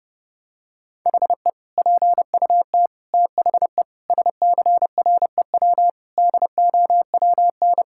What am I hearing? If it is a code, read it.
Morse code, heard as a keyed tone, about 30 wpm, 700 Hz. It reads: HE PUT THE SCREW DOWN
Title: Sentences from Top 1600 Words 30wpm